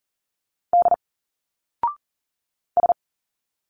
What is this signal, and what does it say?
Morse code, heard as a keyed tone, about 40 wpm, 700 Hz. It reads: D S
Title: Single Letters 40wpm